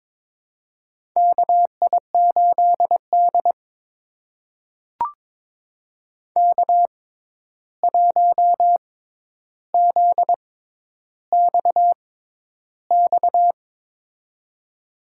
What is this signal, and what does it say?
Morse code, heard as a keyed tone, about 23 wpm, spaced as for 11 wpm, 700 Hz. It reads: KI8D K1ZXX
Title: USA Call Signs - 6x Character Spacing - 22wpm